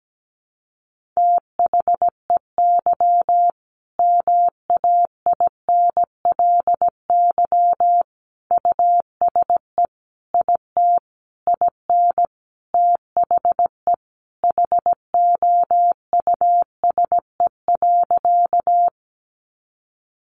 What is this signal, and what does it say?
Morse code, heard as a keyed tone, about 17 wpm, 700 Hz. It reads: THEY MAINLY USE IT IN THE HOUSE.
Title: Sentences from Top 1400 Words 17wpm